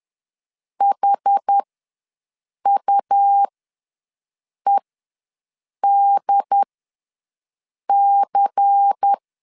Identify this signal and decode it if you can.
Morse code, heard as a keyed tone, about 10 wpm, spaced as for 6 wpm, 800 Hz. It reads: HUEDC
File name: LEZIONI / 8 YJ45 / 8 YJ ws 10wpm 4043